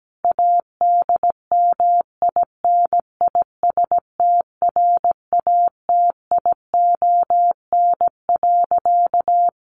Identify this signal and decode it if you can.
Morse code, heard as a keyed tone, about 17 wpm, 700 Hz. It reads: ADMINISTRATION.
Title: Sentences from Top 1300 Words 17wpm